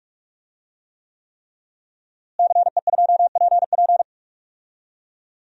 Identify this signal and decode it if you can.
Morse code, heard as a keyed tone, about 41 wpm, 700 Hz. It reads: CE2PP
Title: DXCC-100 Call Signs - 5 Characters - Repeated 3x - 45wpm